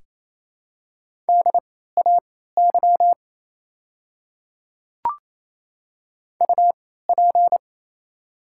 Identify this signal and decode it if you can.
Morse code, heard as a keyed tone, about 28 wpm, spaced as for 16 wpm, 690 Hz. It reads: DAY UP
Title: 3x Character Spacing - Top 100 Words - 28wpm